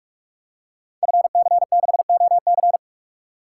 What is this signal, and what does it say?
Morse code, heard as a keyed tone, about 46 wpm, 700 Hz. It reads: FC6OC
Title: Code Groups - 5 Letter:Numbers - 45wpm